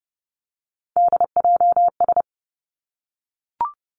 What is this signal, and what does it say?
Morse code, heard as a keyed tone, about 30 wpm, 700 Hz. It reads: DJS
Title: Three Challenging Letter-Number Combinations 30wpm